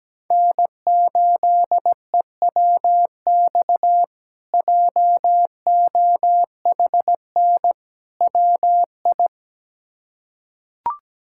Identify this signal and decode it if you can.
Morse code, heard as a keyed tone, about 17 wpm, 700 Hz. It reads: N8EWX JOHN WI
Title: K1USN SST QSO Practice - 17wpm